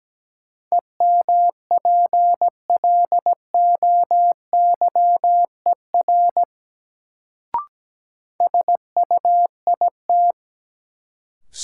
Morse code, heard as a keyed tone, about 17 wpm, 700 Hz. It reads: EMPLOYER SUIT
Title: Sentences from Top 1600 Words - New Word Drill - 17wpm